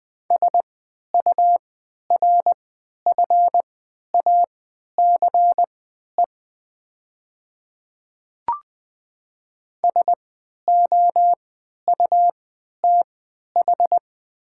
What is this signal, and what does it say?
Morse code, heard as a keyed tone, about 20 wpm, spaced as for 11 wpm, 700 Hz. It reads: SURFACE SOUTH
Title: 3x Character Spacing - Top 500 Words - 20wpm